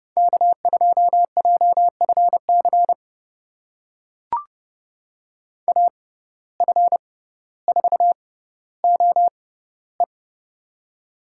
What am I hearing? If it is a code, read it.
Morse code, heard as a keyed tone, about 30 wpm, spaced as for 15 wpm, 700 Hz. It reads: K2JFC AF4OE
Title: USA Call Signs - 5 Characters - 6x Character Spacing - 30wpm